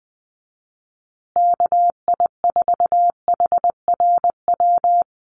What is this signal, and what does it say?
Morse code, heard as a keyed tone, about 20 wpm, 700 Hz. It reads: KI4HRW
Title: USA Call Signs Repeated 3x 20wpm